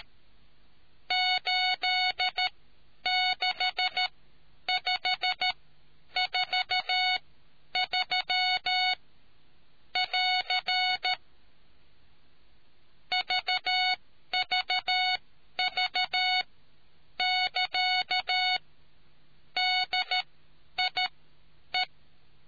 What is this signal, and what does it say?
Morse code, heard as a keyed tone, about 13 wpm, spaced as for 9 wpm, 740 Hz. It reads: 86543 <AR> VVV<KA> DIE